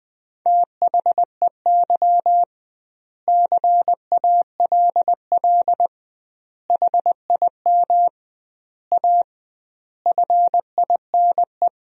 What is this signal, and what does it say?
Morse code, heard as a keyed tone, about 20 wpm, 700 Hz. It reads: THEY CALL HIM A FINE
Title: 2x Word-Spacing - Sentences from Top 500 Words - 20wpm